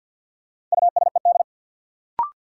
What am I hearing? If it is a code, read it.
Morse code, heard as a keyed tone, about 48 wpm, 700 Hz. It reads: USED
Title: Sentences from Top 600 Words - New Word Drill - 50wpm